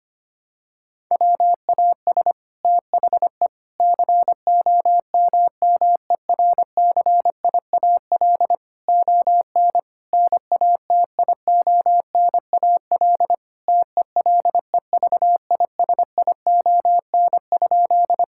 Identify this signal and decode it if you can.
Morse code, heard as a keyed tone, about 25 wpm, 700 Hz. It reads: WAS THE COMMERCIAL ON NATIONAL TELEVISION?
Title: Sentences from Top 1000 Words - Review Part 5 - 25wpm